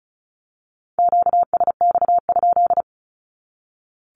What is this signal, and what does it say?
Morse code, heard as a keyed tone, about 34 wpm, 700 Hz. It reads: QSX?
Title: Q-Codes 35wpm